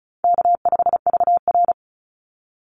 Morse code, heard as a keyed tone, about 34 wpm, 700 Hz. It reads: K5VR
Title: Call Signs Repeated 3x - 35wpm